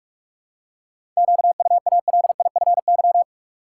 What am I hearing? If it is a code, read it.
Morse code, heard as a keyed tone, about 42 wpm, 700 Hz. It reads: QUALIFY
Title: Seven Letter Words - 45wpm